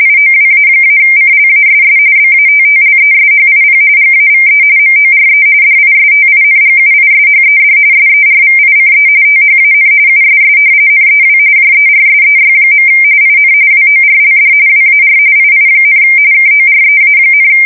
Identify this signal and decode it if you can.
RTTY, about 45.45 baud. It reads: IOWA-CLASS BATTLESHIP GAVE MORE DISTINGUISHED SERVICE THAN USS NEW JERSEY. IN WORLD WAR II, KOREA,
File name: fsk170 / battleshipnj